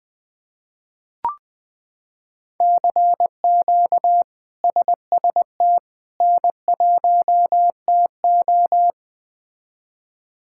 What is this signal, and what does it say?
Morse code, heard as a keyed tone, about 20 wpm, 700 Hz. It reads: CQ SST N1TO
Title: K1USN SST QSO Practice - 20wpm